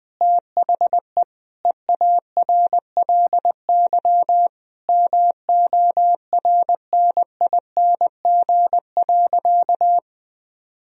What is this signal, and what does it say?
Morse code, heard as a keyed tone, about 20 wpm, 700 Hz. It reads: THE EARLY MORNING.
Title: Sentences from Top 1000 Words - Review Part 3 - 20wpm